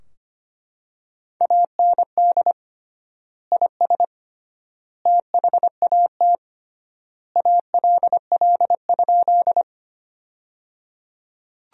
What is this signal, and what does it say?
Morse code, heard as a keyed tone, about 25 wpm, 700 Hz. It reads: AND IS THAT ALL?
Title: 3x Word-Spacing - Sentences from Top 100 Words - 25wpm